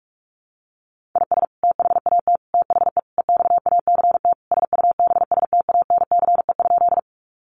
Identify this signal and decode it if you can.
Morse code, heard as a keyed tone, about 44 wpm, 700 Hz. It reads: IS THAT THE EXACT SUBSTANCE?
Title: Sentences from Top 1500 Words 45wpm